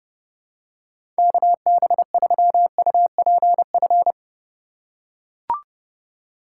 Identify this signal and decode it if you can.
Morse code, heard as a keyed tone, about 30 wpm, 700 Hz. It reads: KB3UPF
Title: USA Call Signs - 6 Characters - 2x Character Spacing - 30wpm